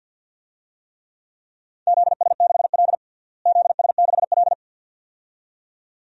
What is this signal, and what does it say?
Morse code, heard as a keyed tone, about 51 wpm, 700 Hz. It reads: ZS6L ZS6L
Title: DXCC-100 Call Signs - 4 Characters - Repeated 2x - 50wpm